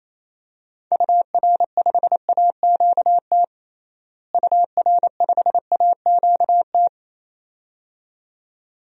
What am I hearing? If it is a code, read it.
Morse code, heard as a keyed tone, about 27 wpm, 700 Hz. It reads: UR5AQT UR5AQT
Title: DXCC-100 Call Signs - 6 Characters - Repeated 2x - 28wpm